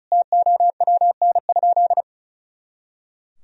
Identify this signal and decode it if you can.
Morse code, heard as a keyed tone, about 34 wpm, 700 Hz. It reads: TOWN?
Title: Sentences from Top 500 Words and Abbreviations 35wpm